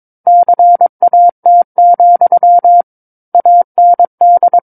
Morse code, heard as a keyed tone, about 22 wpm, 700 Hz. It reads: CAT, AND
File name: 06-Chapter-06 22 WPM